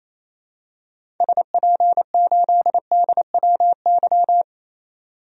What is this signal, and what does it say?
Morse code, heard as a keyed tone, about 28 wpm, 700 Hz. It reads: SP8DWY
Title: DXCC-100 Call Signs - 6 Characters - 28wpm